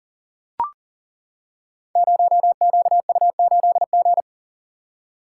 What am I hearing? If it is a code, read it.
Morse code, heard as a keyed tone, about 39 wpm, 690 Hz. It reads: 0QU8G
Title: Code Groups - 5 Letter:Numbers - 40wpm